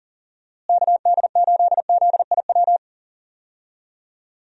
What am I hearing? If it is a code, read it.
Morse code, heard as a keyed tone, about 37 wpm, 700 Hz. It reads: KD8ZIW
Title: USA Call Signs - 6 Characters - Repeated 2x - 40wpm